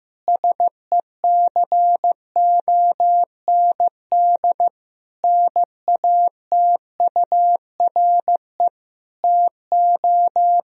Morse code, heard as a keyed tone, about 15 wpm, 700 Hz. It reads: SECOND NATURE TO
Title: Sentences from Top 1000 Words - Review Part 1 - 15wpm